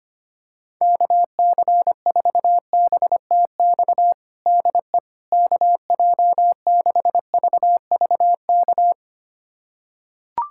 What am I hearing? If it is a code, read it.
Morse code, heard as a keyed tone, about 25 wpm, 700 Hz. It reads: KC4BTX DE KJ6VVK
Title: USA Callsign DE Callsign - 6 Characters - 25wpm